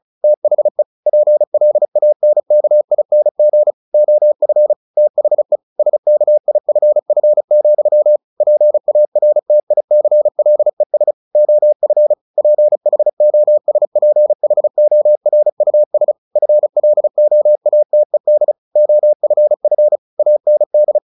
Morse code, heard as a keyed tone, about 34 wpm, 600 Hz. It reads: THE PLANKING OF THE SKIFF, PARTICLES OF PHOSPHORUS FLOATED OFF AND
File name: the old man and the sea 2 35wpm